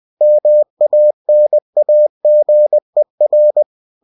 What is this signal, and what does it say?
Morse code, heard as a keyed tone, about 20 wpm, 600 Hz. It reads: MANAGER